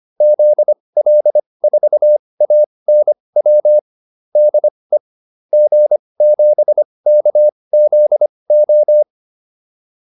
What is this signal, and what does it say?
Morse code, heard as a keyed tone, about 25 wpm, spaced as for 20 wpm, 600 Hz. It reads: ZL4ANW DE G7KZO